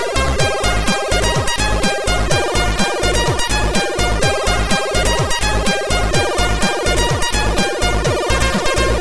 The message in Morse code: none